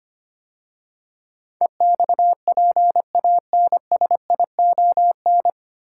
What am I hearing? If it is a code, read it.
Morse code, heard as a keyed tone, about 25 wpm, 700 Hz. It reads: EXPANSION
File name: practice / 25wpm / 9 Letter Words - First 3 Letters Given - 25wpm